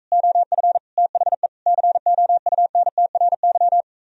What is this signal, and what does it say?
Morse code, heard as a keyed tone, about 45 wpm, 700 Hz. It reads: OF THE COUNTRY